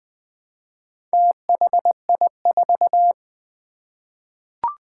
Morse code, heard as a keyed tone, about 20 wpm, 700 Hz. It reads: THI4
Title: Four Challenging Letter-Number Combinations 20wpm